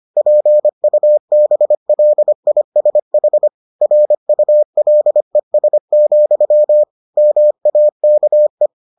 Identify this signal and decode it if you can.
Morse code, heard as a keyed tone, about 25 wpm, 600 Hz. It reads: PUBLISH RULES, MAKE